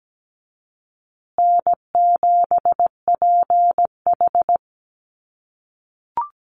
Morse code, heard as a keyed tone, about 17 wpm, 700 Hz. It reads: N7PH
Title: USA Call Signs Repeated 3x 17wpm